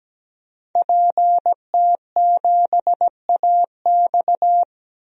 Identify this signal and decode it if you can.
Morse code, heard as a keyed tone, about 17 wpm, 700 Hz. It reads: PT7AX
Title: DXCC-100 Call Signs - 5 Characters - 17wpm